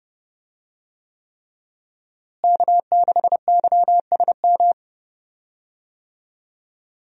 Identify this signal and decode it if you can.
Morse code, heard as a keyed tone, about 30 wpm, 700 Hz. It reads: K6YSM
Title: USA Call Signs - 5 Characters - Repeated 3x - 30wpm